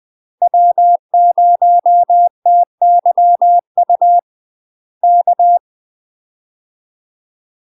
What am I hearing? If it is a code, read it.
Morse code, heard as a keyed tone, about 20 wpm, 700 Hz. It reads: W0TYU K
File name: UK MASTER QSOs TRAINING / 20WPM / UK MASTER QSOs 0073